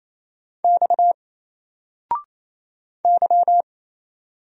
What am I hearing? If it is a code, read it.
Morse code, heard as a keyed tone, about 27 wpm, 700 Hz. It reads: X Y
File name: Single-Letter-Number-28wpm